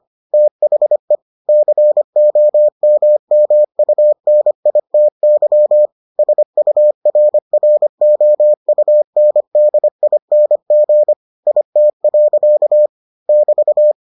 Morse code, heard as a keyed tone, about 25 wpm, 600 Hz. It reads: THE COMMUNITY SURROUNDING IT. <BT>